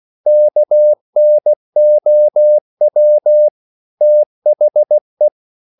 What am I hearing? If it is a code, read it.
Morse code, heard as a keyed tone, about 16 wpm, 600 Hz. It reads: KNOW THE
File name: CW / Practice / EnglishQuotations0216